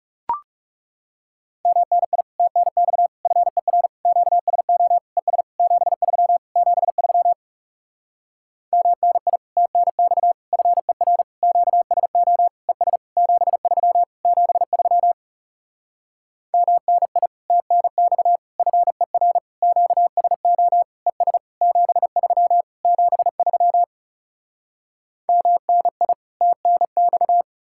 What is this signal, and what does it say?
Morse code, heard as a keyed tone, about 40 wpm, 700 Hz. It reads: MNI TNX FER QSO ES 73 73 MNI TNX FER QSO ES 73 73 MNI TNX FER QSO ES 73 73 MNI TNX